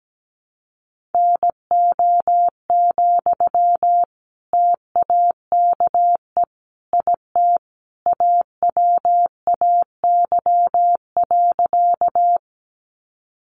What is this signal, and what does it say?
Morse code, heard as a keyed tone, about 17 wpm, 700 Hz. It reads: NO, TAKE IT AWAY.
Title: Sentences from Top 600 Words 17wpm